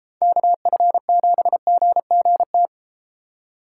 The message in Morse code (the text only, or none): KF7GGT